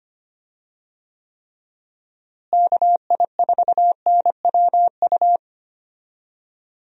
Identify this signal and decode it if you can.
Morse code, heard as a keyed tone, about 25 wpm, 700 Hz. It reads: KI4NWU
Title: USA Call Signs - 6 Characters - Repeated 3x - 25wpm